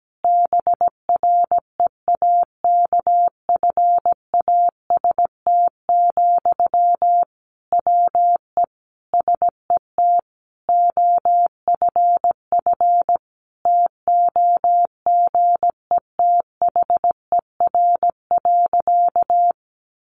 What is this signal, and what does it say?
Morse code, heard as a keyed tone, about 17 wpm, 700 Hz. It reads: BREAKFAST, WE SET OFF TOGETHER.
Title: Sentences from Top 1500 Words 17wpm